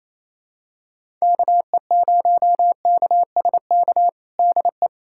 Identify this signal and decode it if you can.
Morse code, heard as a keyed tone, about 28 wpm, 700 Hz. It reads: KE0KSK DE
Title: USA Callsign DE Callsign 28wpm